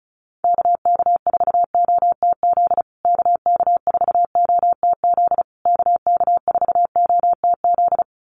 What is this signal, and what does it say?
Morse code, heard as a keyed tone, about 34 wpm, 700 Hz. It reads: KK4OTZ KK4OTZ KK4OTZ
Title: USA Call Signs Repeated 3x 35wpm